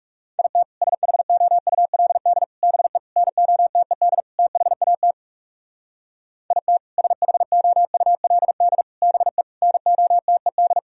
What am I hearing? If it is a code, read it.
Morse code, heard as a keyed tone, about 38 wpm, 700 Hz. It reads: IT SHOULD BE NOTED THAT IT SHOULD BE NOTED